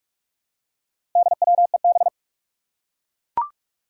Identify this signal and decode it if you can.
Morse code, heard as a keyed tone, about 44 wpm, 700 Hz. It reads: DWEB
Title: Four Challenging Letter-Number Combinations 45wpm